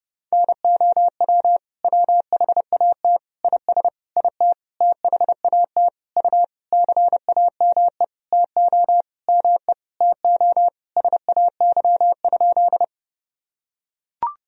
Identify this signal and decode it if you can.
Morse code, heard as a keyed tone, about 32 wpm, 700 Hz. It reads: NOW WHAT IS IT THAT U CAME TO ME TO SAY?
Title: Sentences from Top 200 Words and Abbreviations 30wpm